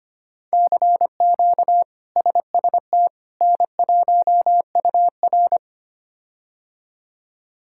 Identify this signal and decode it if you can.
Morse code, heard as a keyed tone, about 25 wpm, 700 Hz. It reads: CQ SST N1UR
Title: K1USN SST QSO Practice - 25wpm